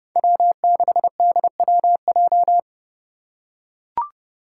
Morse code, heard as a keyed tone, about 28 wpm, 700 Hz. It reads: W6DWJ
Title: Five Challenging Letter-Number Combinations 30wpm